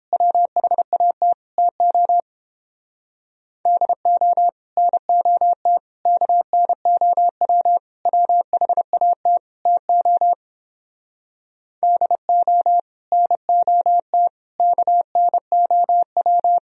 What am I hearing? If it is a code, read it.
Morse code, heard as a keyed tone, about 29 wpm, 700 Hz. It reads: WHAT TO DO NOT KNOW WHAT TO DO NOT KNOW